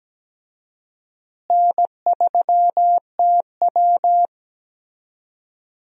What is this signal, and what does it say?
Morse code, heard as a keyed tone, about 17 wpm, 700 Hz. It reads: N3TW
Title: USA Call Signs - 4 Characters - Repeated 3x - 17wpm